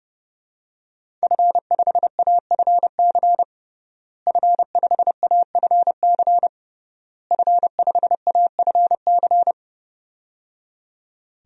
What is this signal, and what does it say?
Morse code, heard as a keyed tone, about 30 wpm, 700 Hz. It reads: F5AFC F5AFC F5AFC
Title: DXCC-100 Call Signs - 5 Characters - Repeated 3x - 30wpm